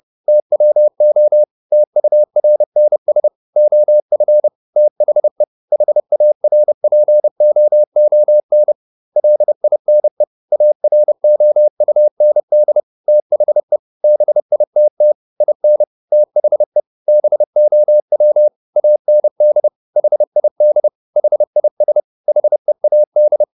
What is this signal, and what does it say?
Morse code, heard as a keyed tone, about 30 wpm, 600 Hz. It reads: TWO TURNS OF THE HARPOON LINE AROUND THE BITT IN THE BOW AND HID HIS HEAD